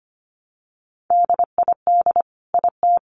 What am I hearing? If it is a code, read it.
Morse code, heard as a keyed tone, about 25 wpm, 700 Hz. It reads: DID IT
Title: Sentences from Top 1200 Words 25wpm